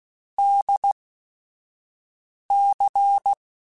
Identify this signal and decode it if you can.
Morse code, heard as a keyed tone, about 16 wpm, spaced as for 5 wpm, 800 Hz. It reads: DC
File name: Technical Subjects / Technical Subjects-mp3s / Sw Pwr Sup @5